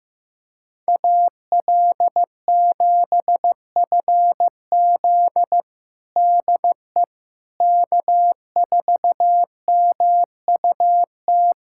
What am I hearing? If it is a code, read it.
Morse code, heard as a keyed tone, about 15 wpm, 700 Hz. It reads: AL7FZ DE K4MUT
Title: USA Callsign DE Callsign - 5 Characters - 15wpm